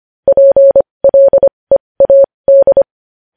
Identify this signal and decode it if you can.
Morse code, heard as a keyed tone, about 25 wpm, spaced as for 20 wpm, 550 Hz. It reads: PLEAD
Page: News Headlines in Morse Code at 20 WPM